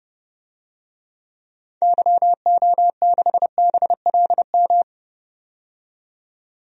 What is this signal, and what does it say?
Morse code, heard as a keyed tone, about 30 wpm, 700 Hz. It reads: YO6BLM
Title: DXCC-100 Call Signs - 6 Characters - 30wpm